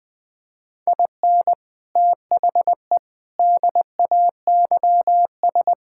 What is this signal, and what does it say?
Morse code, heard as a keyed tone, about 21 wpm, 700 Hz. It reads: IN THE DAYS
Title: Sentences with 7-Word Phrases - 20wpm